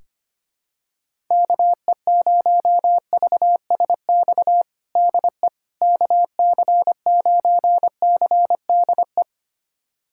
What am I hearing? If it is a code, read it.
Morse code, heard as a keyed tone, about 25 wpm, 700 Hz. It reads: KE0VSX DE KC9CDE
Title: USA Callsign DE Callsign 25wpm